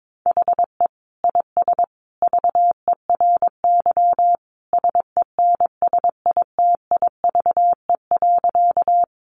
Morse code, heard as a keyed tone, about 22 wpm, 700 Hz. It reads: HE IS VERY SENSITIVE.
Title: Sentences from Top 1400 Words 22wpm